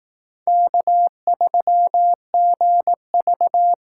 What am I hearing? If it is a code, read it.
Morse code, heard as a keyed tone, about 18 wpm, 700 Hz. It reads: K3GV
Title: Speed Racing - USA 1x2 Call Signs 15wpm